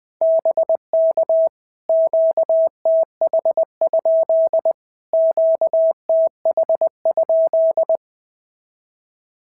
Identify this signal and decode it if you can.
Morse code, heard as a keyed tone, about 18 wpm, 650 Hz. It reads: BK QTH? QTH?
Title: POTA QSO Practice - 20wpm